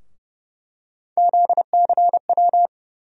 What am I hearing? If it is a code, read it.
Morse code, heard as a keyed tone, about 30 wpm, 700 Hz. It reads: ZCW
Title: USA Call Sign Postfixes - 3 Letters 30wpm